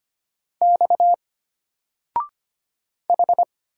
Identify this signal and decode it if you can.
Morse code, heard as a keyed tone, about 25 wpm, 700 Hz. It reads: X H
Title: Single Letter-Number 25wpm